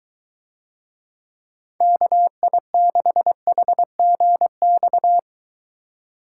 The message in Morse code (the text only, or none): KI6HGX